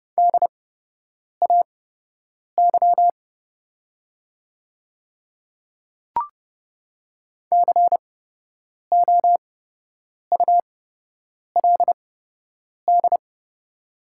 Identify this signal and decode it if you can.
Morse code, heard as a keyed tone, about 30 wpm, spaced as for 8 wpm, 700 Hz. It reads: DAY COULD
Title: 8x Character Spacing - Top 100 Words - 30wpm